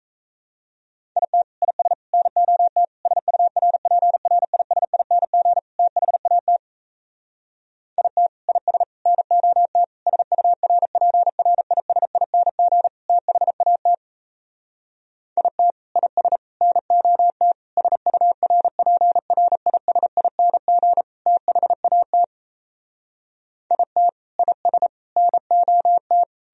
Morse code, heard as a keyed tone, about 38 wpm, 700 Hz. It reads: IT IS NOT SURPRISING THAT IT IS NOT SURPRISING THAT IT IS NOT SURPRISING THAT IT IS NOT